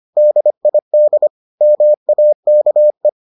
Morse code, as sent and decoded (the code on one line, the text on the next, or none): -.. .. -.. / -- .- -.- .
DID MAKE